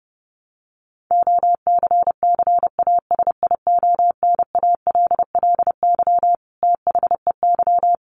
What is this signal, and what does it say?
Morse code, heard as a keyed tone, about 30 wpm, 700 Hz. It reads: OCCASIONALLY THEY
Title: Sentences from Top 1300 Words 30wpm